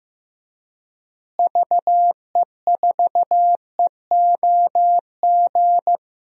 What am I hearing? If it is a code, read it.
Morse code, heard as a keyed tone, about 15 wpm, 700 Hz. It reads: VE4EOG
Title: DXCC-100 Call Signs - 6 Characters - Repeated 3x - 15wpm